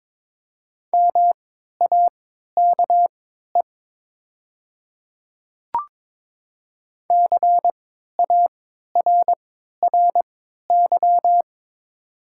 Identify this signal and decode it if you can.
Morse code, heard as a keyed tone, about 22 wpm, spaced as for 12 wpm, 700 Hz. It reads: MAKE CARRY
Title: 3x Character Spacing - Top 500 Words - 22wpm